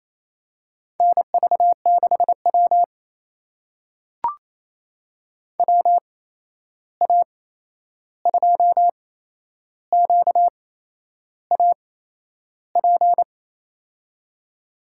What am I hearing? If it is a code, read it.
Morse code, heard as a keyed tone, about 28 wpm, spaced as for 11 wpm, 700 Hz. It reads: NV6W WA2QAP
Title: USA Call Signs - 8x Character Spacing - 28wpm